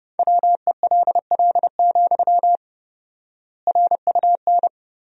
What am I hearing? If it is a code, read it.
Morse code, heard as a keyed tone, about 31 wpm, 700 Hz. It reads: WELL, RUN
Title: 4x Word-Spacing - Sentences from Top 300 Words - 30wpm